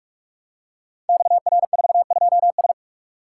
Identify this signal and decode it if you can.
Morse code, heard as a keyed tone, about 44 wpm, 700 Hz. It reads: XR42S
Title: Code Groups - 5 Letter:Numbers - 45wpm